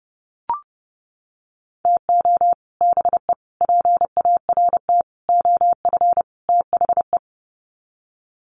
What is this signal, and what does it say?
Morse code, heard as a keyed tone, about 30 wpm, 700 Hz. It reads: TO BE PART OF THE
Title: Sets of 5 Words 30wpm